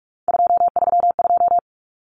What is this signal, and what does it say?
Morse code, heard as a keyed tone, about 44 wpm, 700 Hz. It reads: 232
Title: Numbers - 3-Digit 45wpm